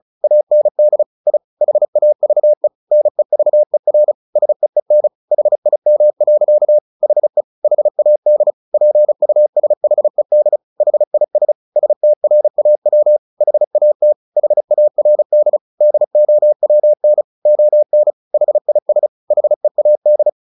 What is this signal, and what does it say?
Morse code, heard as a keyed tone, about 34 wpm, 600 Hz. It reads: AND I HAVE NEVER SEEN HIM. HE HAD PUSHED HIS STRAW HAT HARD DOWN ON HIS HEAD
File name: the old man and the sea / the old man and the sea 1 35wpm